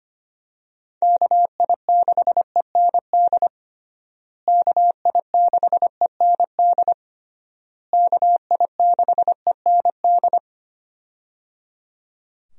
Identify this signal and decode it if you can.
Morse code, heard as a keyed tone, about 25 wpm, 700 Hz. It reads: KI6END KI6END KI6END
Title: USA Call Signs - 6 Characters - Repeated 3x - 25wpm